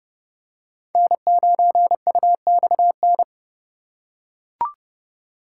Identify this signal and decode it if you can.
Morse code, heard as a keyed tone, about 30 wpm, 700 Hz. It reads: N9UXN
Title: USA Call Signs - 5 Characters - 4x Character Spacing - 30wpm